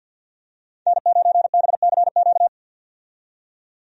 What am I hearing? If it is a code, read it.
Morse code, heard as a keyed tone, about 49 wpm, 700 Hz. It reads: N9BCQ
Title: USA Call Signs - 5 Characters - 50wpm